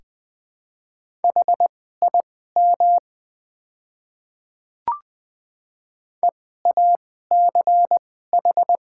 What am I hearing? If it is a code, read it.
Morse code, heard as a keyed tone, about 20 wpm, spaced as for 14 wpm, 700 Hz. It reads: HIM EACH
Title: 2x Character Spacing - Top 100 Words - 20wpm